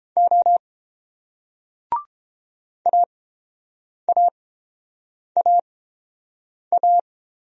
Morse code, heard as a keyed tone, about 31 wpm, spaced as for 7 wpm, 700 Hz. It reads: O AAAA